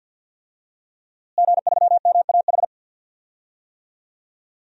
Morse code, heard as a keyed tone, about 53 wpm, 700 Hz. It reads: G3MAH
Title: DXCC-100 Call Signs - 5 Characters - Repeated 2x - 50wpm